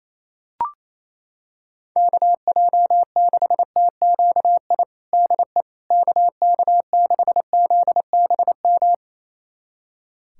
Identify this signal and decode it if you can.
Morse code, heard as a keyed tone, about 28 wpm, 700 Hz. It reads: KJ6TQI DE KK6ZBM
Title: USA Callsign DE Callsign 28wpm